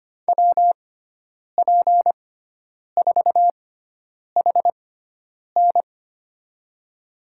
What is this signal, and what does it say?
Morse code, heard as a keyed tone, about 25 wpm, spaced as for 9 wpm, 700 Hz. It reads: WP4HN
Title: USA Call Signs - 5 Characters - 6x Character Spacing - 25wpm